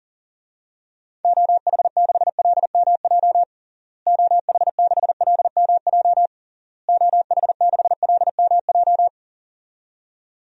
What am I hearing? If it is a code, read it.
Morse code, heard as a keyed tone, about 37 wpm, 700 Hz. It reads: OH6LMJ OH6LMJ OH6LMJ
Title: DXCC-100 Call Signs - 6 Characters - Repeated 3x - 40wpm